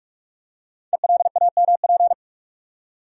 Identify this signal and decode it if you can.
Morse code, heard as a keyed tone, about 44 wpm, 690 Hz. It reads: ELAMP
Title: Code Groups - 5 Letters - 45wpm